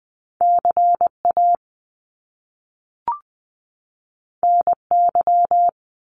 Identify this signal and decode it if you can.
Morse code, heard as a keyed tone, about 20 wpm, 700 Hz. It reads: CA NY
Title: US States 20wpm